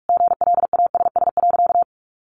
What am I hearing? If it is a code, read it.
Morse code, heard as a keyed tone, about 44 wpm, 700 Hz. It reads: GLASS.
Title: Sentences from Top 1300 Words 45wpm